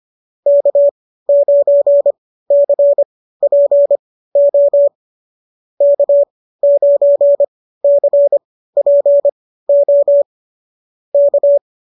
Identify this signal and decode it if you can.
Morse code, heard as a keyed tone, about 25 wpm, spaced as for 15 wpm, 590 Hz. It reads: K9CPO K9CPO K
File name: qso211 15